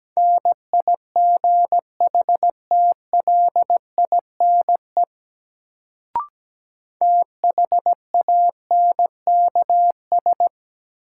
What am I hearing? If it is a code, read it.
Morse code, heard as a keyed tone, about 17 wpm, 700 Hz. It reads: NIGHTLINE THANKS
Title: Sets of 5 Words - New Word Drill - 17wpm